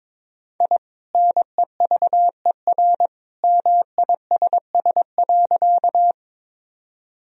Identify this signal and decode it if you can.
Morse code, heard as a keyed tone, about 22 wpm, 700 Hz. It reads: I NEVER MISS.
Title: Sentences from Top 1000 Words - Review Part 3 - 22wpm